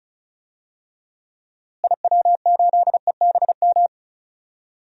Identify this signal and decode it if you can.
Morse code, heard as a keyed tone, about 34 wpm, 700 Hz. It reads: IW8EBM
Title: DXCC-100 Call Signs - 6 Characters - Repeated 3x - 35wpm